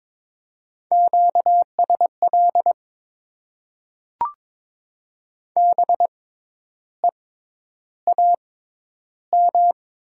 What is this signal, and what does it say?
Morse code, heard as a keyed tone, about 22 wpm, spaced as for 10 wpm, 700 Hz. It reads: QSL BEAM
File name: Top 90 QSO Elements - 6x Character Spacing - 22wpm